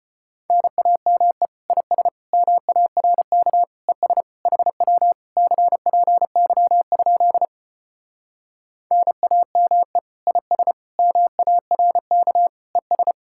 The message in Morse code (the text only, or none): NAME IS MARK ES HW CPY? NAME IS MARK ES